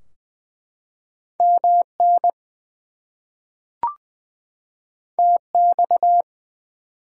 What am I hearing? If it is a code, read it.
Morse code, heard as a keyed tone, about 20 wpm, 700 Hz. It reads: MN TX
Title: US States and Canadian Provinces 20wpm